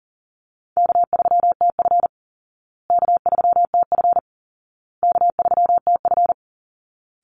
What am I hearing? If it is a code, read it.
Morse code, heard as a keyed tone, about 40 wpm, 700 Hz. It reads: K3TF K3TF K3TF
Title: Easy Call Signs Repeated 3x - 40wpm